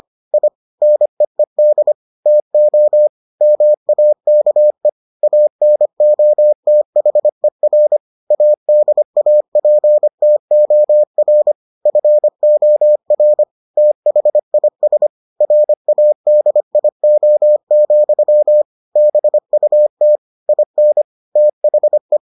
Morse code, heard as a keyed tone, about 25 wpm, 600 Hz. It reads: I NEED TO MAKE ANOTHER ADAPTOR FOR THIS RADIO, BUT IN THE